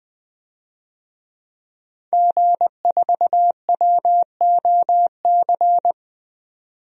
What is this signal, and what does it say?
Morse code, heard as a keyed tone, about 20 wpm, 700 Hz. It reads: G4WOC
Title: DXCC-100 Call Signs - 5 Characters - 20wpm